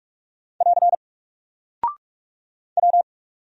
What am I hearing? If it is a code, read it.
Morse code, heard as a keyed tone, about 44 wpm, 700 Hz. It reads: <AR> W
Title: Single Character (Letter:Number:Punctuation:Prosign) 45wpm